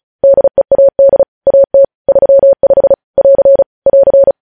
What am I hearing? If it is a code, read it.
Morse code, heard as a keyed tone, about 36 wpm, 550 Hz. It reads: DEAD AT 35 <AR> <AR>
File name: podcast / News headlines-1x-35x35